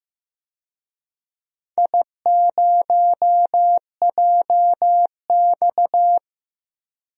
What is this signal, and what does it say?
Morse code, heard as a keyed tone, about 15 wpm, 700 Hz. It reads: I0JX
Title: DXCC-100 Call Signs - 4 Characters - 15wpm